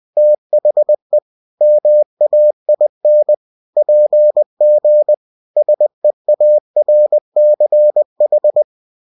Morse code, heard as a keyed tone, about 20 wpm, 600 Hz. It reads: THE MAIN PG SEARCH
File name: cw-books / winnie-the-pooh / winnie-20wpm-page-0122